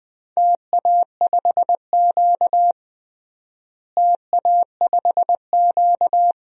Morse code, heard as a keyed tone, about 20 wpm, 700 Hz. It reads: TA5Q TA5Q
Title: DXCC-100 Call Signs - 4 Characters - Repeated 2x - 20wpm